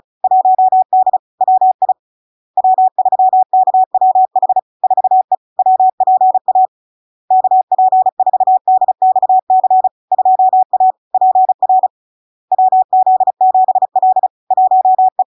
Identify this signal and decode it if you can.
Morse code, heard as a keyed tone, about 36 wpm, 750 Hz. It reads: 1D WI W3KWH 4E WPA KP4DXC 2A PR WZ7L 1E
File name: LARK-FD-2022-Log-in-CW-35wpm-750-Hz